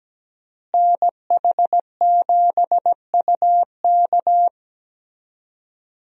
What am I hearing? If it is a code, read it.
Morse code, heard as a keyed tone, about 17 wpm, 700 Hz. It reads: NH7UK
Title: USA Call Signs - 5 Characters - Repeated 2x - 17wpm